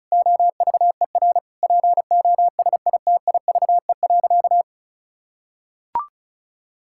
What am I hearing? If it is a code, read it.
Morse code, heard as a keyed tone, about 36 wpm, 700 Hz. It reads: OVER POSITIVE.
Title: Sentences from Top 1000 Words - Review Part 5 - 35wpm